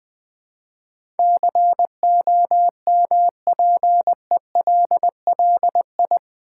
Morse code, heard as a keyed tone, about 20 wpm, 700 Hz. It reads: COMPELLI
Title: Build A Word - 4 Letters At A Time - 10-Letter Words - 20wpm